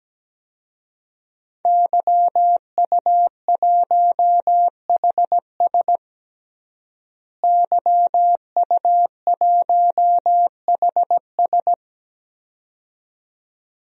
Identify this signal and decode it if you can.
Morse code, heard as a keyed tone, about 17 wpm, 700 Hz. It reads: YU1HS YU1HS
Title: DXCC-100 Call Signs - 5 Characters - Repeated 2x - 17wpm